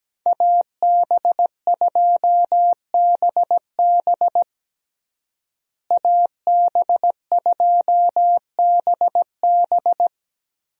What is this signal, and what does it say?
Morse code, heard as a keyed tone, about 17 wpm, 700 Hz. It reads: AB2BB AB2BB
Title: USA Call Signs - 5 Characters - Repeated 3x - 17wpm